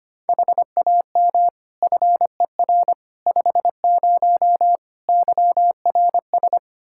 Morse code, heard as a keyed tone, about 25 wpm, 700 Hz. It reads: HAM FER 50 YRS
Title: Practice QSO - Exchange 3 - More Info - 25wpm